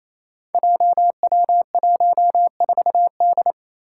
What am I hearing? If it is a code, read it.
Morse code, heard as a keyed tone, about 27 wpm, 700 Hz. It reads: JW14D
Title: Five Challenging Letter-Number Combinations 28wpm